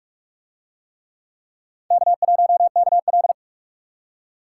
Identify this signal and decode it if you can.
Morse code, heard as a keyed tone, about 44 wpm, 700 Hz. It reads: K1KL